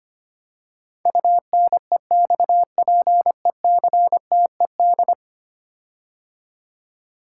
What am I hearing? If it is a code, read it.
Morse code, heard as a keyed tone, about 25 wpm, 700 Hz. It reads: UNEXPECTED
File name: 10 Letter Words - First 3 Letters Given - 25wpm